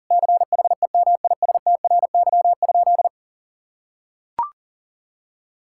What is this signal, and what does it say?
Morse code, heard as a keyed tone, about 40 wpm, 700 Hz. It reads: CHEMISTRY?
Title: Sentences from Top 1000 Words - Review Part 6 - 40wpm